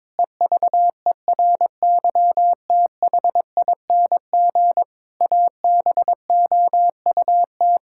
Morse code, heard as a keyed tone, about 22 wpm, 700 Hz. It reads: EVERYTHING ABOUT
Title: Pangram Challenge v2 - 22wpm